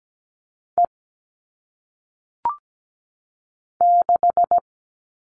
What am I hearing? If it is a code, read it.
Morse code, heard as a keyed tone, about 17 wpm, 700 Hz. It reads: E 6
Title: All Cut Numbers - 1-Digit 17wpm